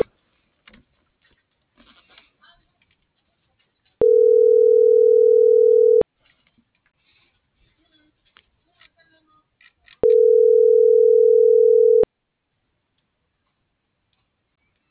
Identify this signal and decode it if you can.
Ringback tone (North America), 440 + 480 Hz.